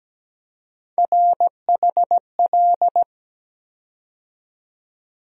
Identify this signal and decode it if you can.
Morse code, heard as a keyed tone, about 17 wpm, 700 Hz. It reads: RHL
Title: USA Call Sign Postfixes - 3 Letters 17wpm